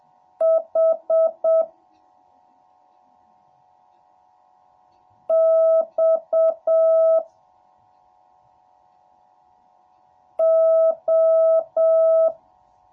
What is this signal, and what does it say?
Morse code, heard as a keyed tone, about 7 wpm, 650 Hz. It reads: H X O